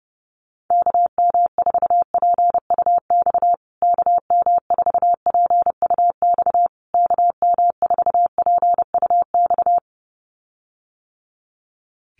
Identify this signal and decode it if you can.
Morse code, heard as a keyed tone, about 28 wpm, 700 Hz. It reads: KM4PUX KM4PUX KM4PUX
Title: USA Call Signs Repeated 3x 30wpm